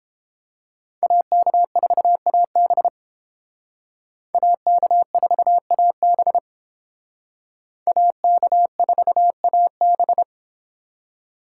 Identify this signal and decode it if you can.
Morse code, heard as a keyed tone, about 30 wpm, 700 Hz. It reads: AK4AB AK4AB AK4AB